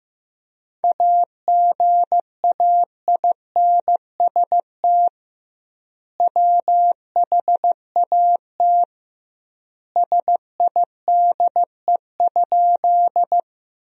Morse code, heard as a keyed tone, about 16 wpm, 700 Hz. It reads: AGAINST WHAT SIDE?
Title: 2x Word-Spacing - Sentences from Top 400 Words - 15wpm